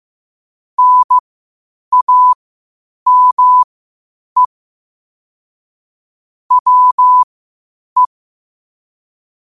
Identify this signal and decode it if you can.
Morse code, heard as a keyed tone, about 15 wpm, spaced as for 8 wpm, 1000 Hz. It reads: NAME WE
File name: qotdfiles / 8wpm